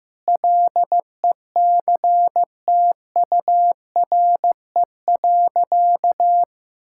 Morse code, heard as a keyed tone, about 15 wpm, 700 Hz. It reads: LECTURE.
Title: Sentences from Top 1600 Words 15wpm